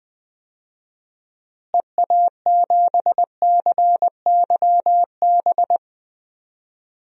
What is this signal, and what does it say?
Morse code, heard as a keyed tone, about 20 wpm, 700 Hz. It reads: EA7CYB
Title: DXCC-100 Call Signs - 6 Characters - Repeated 3x - 20wpm